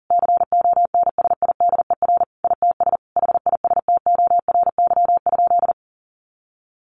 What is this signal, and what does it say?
Morse code, heard as a keyed tone, about 40 wpm, 700 Hz. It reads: CONSIDER ITS HISTORY?
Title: Sentences from Top 1100 Words 40wpm